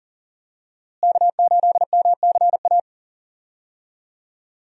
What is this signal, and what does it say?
Morse code, heard as a keyed tone, about 37 wpm, 700 Hz. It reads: K8MCA
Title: USA Call Signs - 5 Characters - Repeated 2x - 40wpm